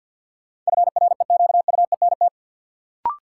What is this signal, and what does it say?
Morse code, heard as a keyed tone, about 53 wpm, 700 Hz. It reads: FREQUENT
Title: Sentences from Top 1400 Words - New Word Drill Alt - 50wpm